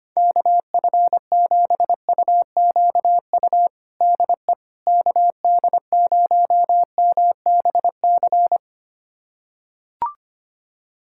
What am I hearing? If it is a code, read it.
Morse code, heard as a keyed tone, about 25 wpm, 700 Hz. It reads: KF7UQU DE KD0MBC
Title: USA Callsign DE Callsign - 6 Characters - 25wpm